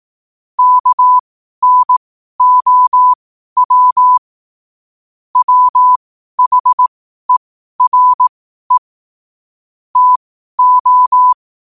Morse code, heard as a keyed tone, about 18 wpm, spaced as for 12 wpm, 1000 Hz. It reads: KNOW WHERE TO
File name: qotdfiles / 12wpm